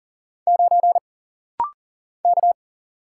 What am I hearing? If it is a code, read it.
Morse code, heard as a keyed tone, about 40 wpm, 700 Hz. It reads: 9 K
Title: Single Letter-Number - Rapid-Fire 40wpm